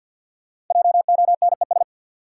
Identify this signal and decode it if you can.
Morse code, heard as a keyed tone, about 47 wpm, 700 Hz. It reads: JONES
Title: Sets of 5 Words - New Word Drill - 50wpm